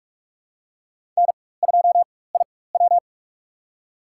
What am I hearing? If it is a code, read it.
Morse code, heard as a keyed tone, about 46 wpm, spaced as for 21 wpm, 700 Hz. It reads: N2IW
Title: USA Call Signs - 4 Characters - 4x Character Spacing - 45wpm